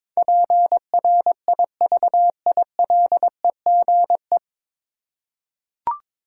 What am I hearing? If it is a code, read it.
Morse code, heard as a keyed tone, about 22 wpm, 690 Hz. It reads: PRIVILEGE
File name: 9 Letter Words - First 3 Letters Given - 22wpm